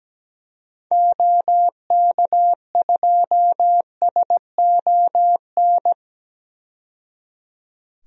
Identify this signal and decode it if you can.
Morse code, heard as a keyed tone, about 17 wpm, 700 Hz. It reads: OK2SON
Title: DXCC-100 Call Signs - 6 Characters - 17wpm